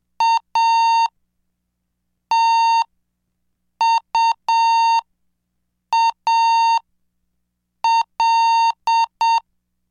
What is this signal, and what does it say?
Morse code, heard as a keyed tone, about 7 wpm, spaced as for 5 wpm, 920 Hz. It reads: ATUAL
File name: cursos / CDestudo / 26 Track 26